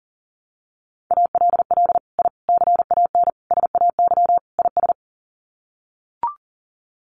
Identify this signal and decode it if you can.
Morse code, heard as a keyed tone, about 40 wpm, 700 Hz. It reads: ALL I CAN SAY IS
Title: Sets of 5 Words Encore2 40wpm